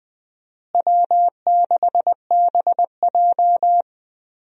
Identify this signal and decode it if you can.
Morse code, heard as a keyed tone, about 21 wpm, 700 Hz. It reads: W6BJ
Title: USA Call Signs - 4 Characters - Repeated 2x - 20wpm